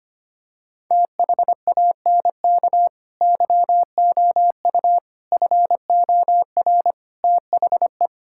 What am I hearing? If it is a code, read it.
Morse code, heard as a keyed tone, about 25 wpm, 700 Hz. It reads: THANK YOU FOR THE
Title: Sentences from Top 1600 Words 25wpm